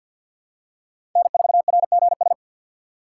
Morse code, heard as a keyed tone, about 47 wpm, 700 Hz. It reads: N4RGS
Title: USA Call Signs - 5 Characters - 6x Character Spacing - 50wpm